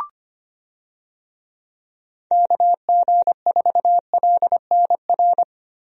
Morse code, heard as a keyed tone, about 25 wpm, 700 Hz. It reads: KG4LNR
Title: USA Call Signs - 6 Characters - Repeated 3x - 25wpm